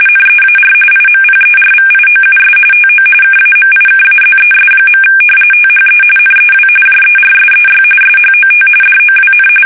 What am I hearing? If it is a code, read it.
RTTY, about 45.45 baud. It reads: NEAR IMPERVIOUS TO KAMIKAZES. THE NEW JERSEY SHOT DOWN
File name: fsk850 / battleshipnj